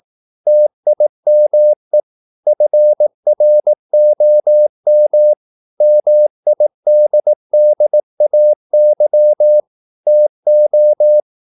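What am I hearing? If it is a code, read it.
Morse code, heard as a keyed tone, about 18 wpm, 600 Hz. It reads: TIME FROM MIDDAY TO
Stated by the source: Morse practice at 18 wpm, English text, from CW med Gnister.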